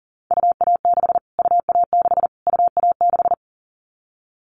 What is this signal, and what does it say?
Morse code, heard as a keyed tone, about 40 wpm, 700 Hz. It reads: UA6 UA6 UA6
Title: Callsign Prefixes for Top 25 DXCC - Repeated 3x - 40wpm